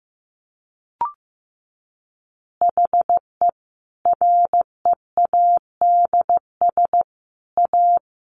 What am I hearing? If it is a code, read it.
Morse code, heard as a keyed tone, about 15 wpm, 700 Hz. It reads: HE READS A
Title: Additional Easy Phrases 15wpm